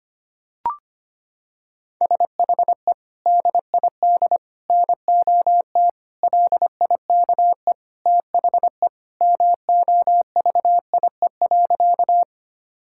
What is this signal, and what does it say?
Morse code, heard as a keyed tone, about 25 wpm, 700 Hz. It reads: SHE DID NOT LIKE THE MOVIE.
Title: Sentences from Top 1000 Words - Review Part 5 - 25wpm